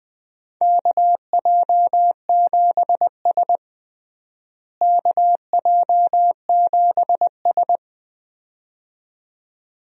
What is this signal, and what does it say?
Morse code, heard as a keyed tone, about 20 wpm, 700 Hz. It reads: KJ7S KJ7S
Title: USA Call Signs - 4 Characters - Repeated 2x - 20wpm